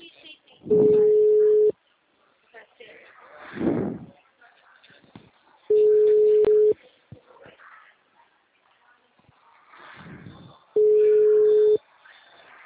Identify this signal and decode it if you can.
Ringback tone (Europe, 425 Hz), with speech.